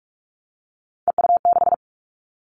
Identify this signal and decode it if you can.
Morse code, heard as a keyed tone, about 44 wpm, 700 Hz. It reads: EU6
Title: Callsign Prefixes for Top 100 DXCC - 45wpm